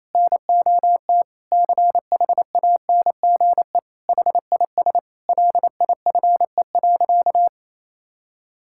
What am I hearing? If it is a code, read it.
Morse code, heard as a keyed tone, about 28 wpm, 700 Hz. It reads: NOT CHANGE HIS LIFE.
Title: Sentences from Top 1000 Words - Review Part 5 - 28wpm